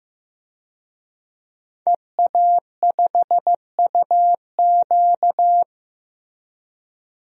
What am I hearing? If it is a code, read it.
Morse code, heard as a keyed tone, about 15 wpm, 700 Hz. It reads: EA5UQ
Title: DXCC-100 Call Signs - 5 Characters - Repeated 2x - 15wpm